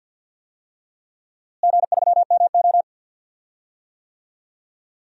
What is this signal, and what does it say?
Morse code, heard as a keyed tone, about 45 wpm, 700 Hz. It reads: G3MO
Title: DXCC-100 Call Signs - 4 Characters - Repeated 3x - 50wpm